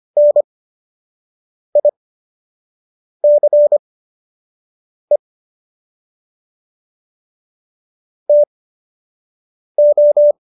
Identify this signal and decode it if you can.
Morse code, heard as a keyed tone, about 25 wpm, spaced as for 6 wpm, 590 Hz. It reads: NICE TO